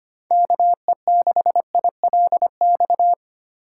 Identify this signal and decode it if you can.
Morse code, heard as a keyed tone, about 25 wpm, 700 Hz. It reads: KE6ILX
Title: USA Call Signs - 6 Characters - 2x Character Spacing - 25wpm